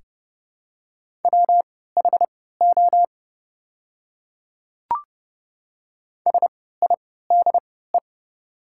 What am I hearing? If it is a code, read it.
Morse code, heard as a keyed tone, about 30 wpm, spaced as for 17 wpm, 700 Hz. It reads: WHO SIDE
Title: 3x Character Spacing - Top 200 Words - 30wpm